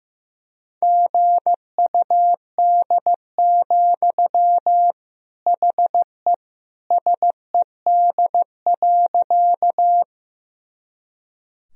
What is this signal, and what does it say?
Morse code, heard as a keyed tone, about 15 wpm, 700 Hz. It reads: GUD, HE SED.
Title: Sentences from Top 200 Words and Abbreviations 15wpm